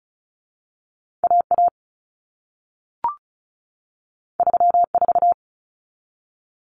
Morse code, heard as a keyed tone, about 36 wpm, 700 Hz. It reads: AA 34
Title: Common Cut Numbers - 2-Digit 35wpm